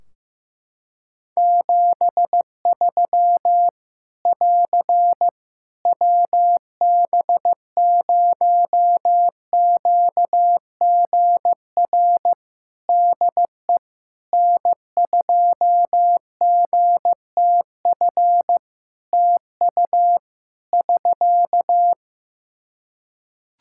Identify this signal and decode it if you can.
Morse code, heard as a keyed tone, about 15 wpm, 700 Hz. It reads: 73 <AR> WB0QGR DE N2GTF TU <SK>
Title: Practice QSO - Exchange 4 - Ending - 15wpm